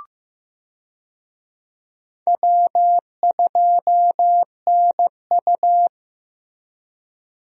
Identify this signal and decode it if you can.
Morse code, heard as a keyed tone, about 16 wpm, 700 Hz. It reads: W2NU
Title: USA Call Signs - 4 Characters - Repeated 3x - 15wpm